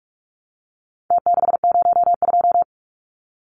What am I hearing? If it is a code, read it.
Morse code, heard as a keyed tone, about 44 wpm, 700 Hz. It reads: T602
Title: Numbers - Contest Style - 4-Digit 45wpm